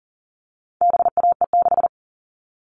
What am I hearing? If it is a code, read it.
Morse code, heard as a keyed tone, about 40 wpm, 700 Hz. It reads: BAE6
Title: All Cut Numbers - 4-Digit 40wpm